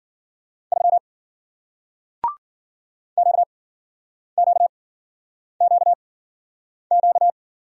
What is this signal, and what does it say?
Morse code, heard as a keyed tone, about 53 wpm, spaced as for 9 wpm, 700 Hz. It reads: 3 QQQQ